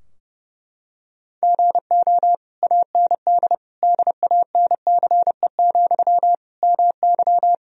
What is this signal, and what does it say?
Morse code, heard as a keyed tone, about 30 wpm, 700 Hz. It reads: GO AND DANCE, MY
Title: Sentences from Top 1000 Words - Review Part 4 - 30wpm